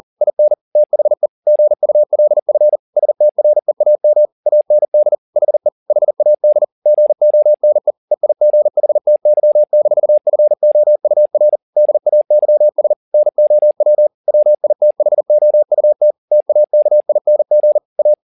Morse code, heard as a keyed tone, about 40 wpm, 600 Hz. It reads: IN THE GULF STREAM AND HE HAD GONE EIGHTY-FOUR DAYS NOW WITHOUT TAKING A
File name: the old man and the sea 1 40wpm